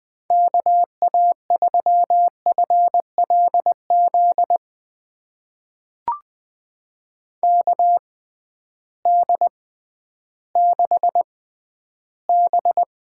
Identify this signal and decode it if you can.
Morse code, heard as a keyed tone, about 20 wpm, spaced as for 12 wpm, 700 Hz. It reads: KA3FLZ KD6B
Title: USA Call Signs - 6 Characters - 6x Character Spacing - 20wpm